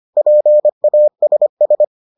Morse code, heard as a keyed tone, about 25 wpm, 600 Hz. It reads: PASS